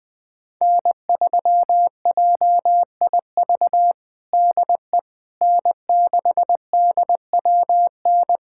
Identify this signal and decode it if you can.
Morse code, heard as a keyed tone, about 20 wpm, 700 Hz. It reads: N3JIV DE N6DWN
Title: USA Callsign DE Callsign - 5 Characters - 20wpm